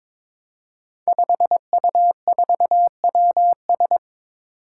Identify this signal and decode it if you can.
Morse code, heard as a keyed tone, about 22 wpm, 700 Hz. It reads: 5U4WS
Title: Five Challenging Letter-Number Combinations 22wpm